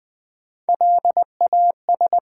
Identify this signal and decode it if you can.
Morse code, heard as a keyed tone, about 20 wpm, 700 Hz. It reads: LAS
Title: Build A Word - 1 Letter At A Time - 5-Letter Words - 20wpm